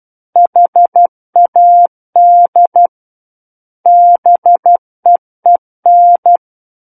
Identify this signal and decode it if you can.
Morse code, heard as a keyed tone, about 12 wpm, 700 Hz. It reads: HAD BEEN
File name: Book One / 03-Chapter 3 - 12 WPM